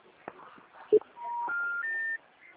Special information tone.